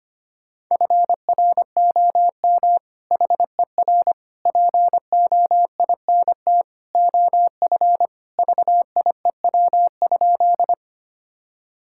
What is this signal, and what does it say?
Morse code, heard as a keyed tone, about 25 wpm, 700 Hz. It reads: FROM HER POINT OF VIEW?
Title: Sentences from Top 1000 Words - Review Part 6 - 25wpm